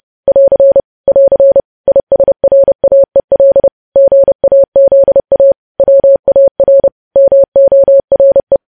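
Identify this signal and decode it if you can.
Morse code, heard as a keyed tone, about 30 wpm, 550 Hz. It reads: <AR> <AR> ISRAEL GAZA WAR MORE
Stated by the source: Morse code sending of news headlines at 30 wpm.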